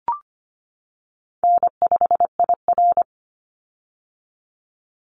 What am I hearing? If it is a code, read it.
Morse code, heard as a keyed tone, about 25 wpm, 700 Hz. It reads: N5IR
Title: Call Signs - 25wpm